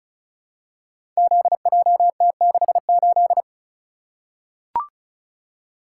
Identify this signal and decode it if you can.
Morse code, heard as a keyed tone, about 35 wpm, 700 Hz. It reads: ZJT68
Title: Code Groups - 5 Letter:Numbers - 35wpm